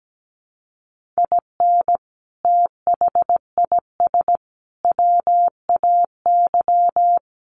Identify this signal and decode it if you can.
Morse code, heard as a keyed tone, about 17 wpm, 700 Hz. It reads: IN THIS WAY
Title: Sentences from Top 1200 Words 17wpm